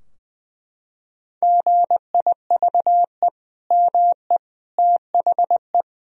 Morse code, heard as a keyed tone, about 21 wpm, 700 Hz. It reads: GIVE ME THE
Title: Sentences from Top 1000 Words - Review Part 2 - 20wpm